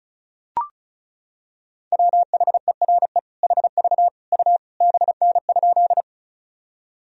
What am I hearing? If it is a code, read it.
Morse code, heard as a keyed tone, about 36 wpm, 700 Hz. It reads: WHERE HV U BN?
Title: Sentences from Top 200 Words and Abbreviations 35wpm